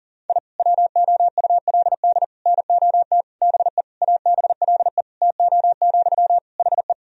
Morse code, heard as a keyed tone, about 40 wpm, 700 Hz. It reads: I WOULD NOT BE ABLE TO, HE
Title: Sentences from Top 1000 Words - Review Part 2 - 40wpm